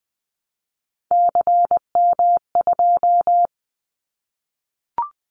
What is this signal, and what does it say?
Morse code, heard as a keyed tone, about 20 wpm, 700 Hz. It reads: CM2
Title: Callsign Prefixes for Top 100 DXCC - 20wpm